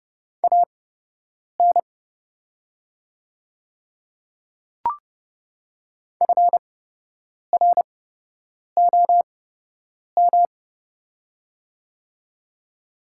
Morse code, heard as a keyed tone, about 30 wpm, spaced as for 8 wpm, 700 Hz. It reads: AN FROM
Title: 8x Character Spacing - Top 100 Words - 30wpm